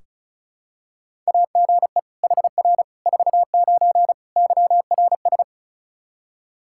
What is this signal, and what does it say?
Morse code, heard as a keyed tone, about 36 wpm, 700 Hz. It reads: AGE HR 49 YRS
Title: K1USN SST QSO Practice 35wpm